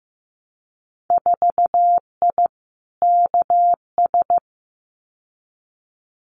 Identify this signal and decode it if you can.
Morse code, heard as a keyed tone, about 15 wpm, 700 Hz. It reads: 4I KS
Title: Winter Field Day Exchanges Encore 15wpm